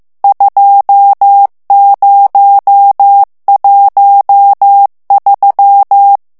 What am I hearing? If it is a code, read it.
Morse code, heard as a keyed tone, about 15 wpm, 800 Hz. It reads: 2013
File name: morse mp3 / ITALIEN texte clair 1 vitesse 14 mots minute